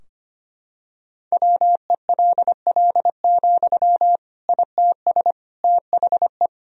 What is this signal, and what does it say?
Morse code, heard as a keyed tone, about 25 wpm, 700 Hz. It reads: WELL, ITS THE
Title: Sentences from Top 1000 Words - Review Part 4 - 25wpm